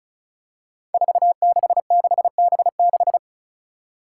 Morse code, heard as a keyed tone, about 36 wpm, 700 Hz. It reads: 466B6
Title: Five Challenging Letter-Number Combinations 35wpm